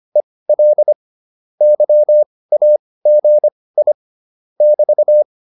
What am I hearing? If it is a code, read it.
Morse code, heard as a keyed tone, about 25 wpm, spaced as for 18 wpm, 600 Hz. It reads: EL YAGI <BT>
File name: QSO202 18